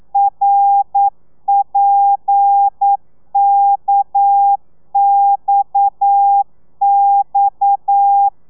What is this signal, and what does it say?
Morse code, heard as a keyed tone, about 9 wpm, 800 Hz. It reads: RPKXX
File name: telegrafia / i2tfj / Lez7 40